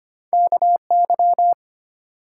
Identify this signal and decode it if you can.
Morse code, heard as a keyed tone, about 25 wpm, 700 Hz. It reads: KY
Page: morse-code-ninja-practice-25wpm